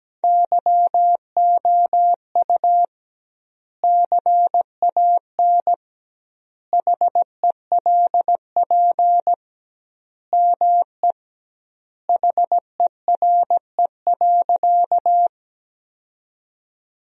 Morse code, heard as a keyed tone, about 17 wpm, 700 Hz. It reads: YOU CAN HELP ME HERE.
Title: 2x Word-Spacing - Sentences from Top 200 Words - 17wpm